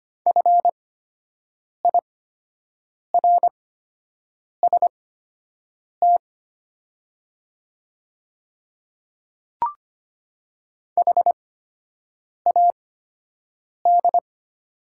Morse code, heard as a keyed tone, about 25 wpm, spaced as for 7 wpm, 690 Hz. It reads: FIRST HAD
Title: 8x Character Spacing - Top 300 Words - 25wpm